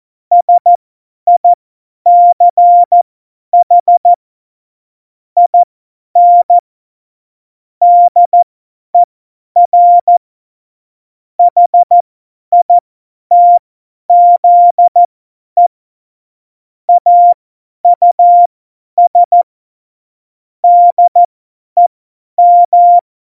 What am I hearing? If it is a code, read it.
Morse code, heard as a keyed tone, about 14 wpm, spaced as for 10 wpm, 700 Hz. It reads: SICH IN DER HITZE AUS DEM
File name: Morsekurs / Witze / Witz 12. November 2019, 50 F70